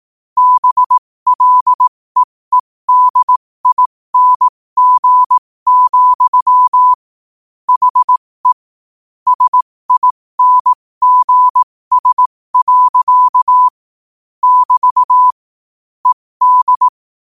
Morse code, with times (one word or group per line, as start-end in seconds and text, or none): 0.4-6.9 BLEEDING,
7.7-8.5 HE
9.3-13.7 SINGS.
14.4-15.3 <BT>
16.1-16.9 ED